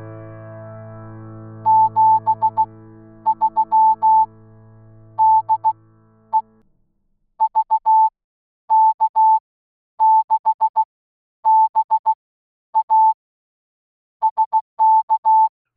Morse code, heard as a keyed tone, about 16 wpm, spaced as for 10 wpm, 880 Hz. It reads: 73 DE VK6BA SK
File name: newswest